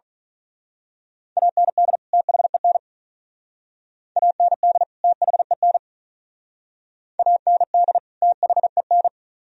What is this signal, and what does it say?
Morse code, heard as a keyed tone, about 42 wpm, 700 Hz. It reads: AND THEN AND THEN AND THEN
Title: Speed-Racing - Sets of 2 Words 35wpm